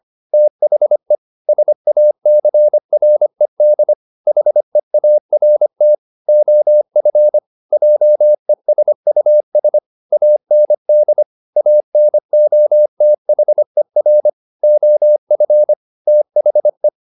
Morse code, heard as a keyed tone, about 25 wpm, 600 Hz. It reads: THE SACRED HEART OF JESUS AND ANOTHER OF THE
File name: the old man and the sea part 1 25wpm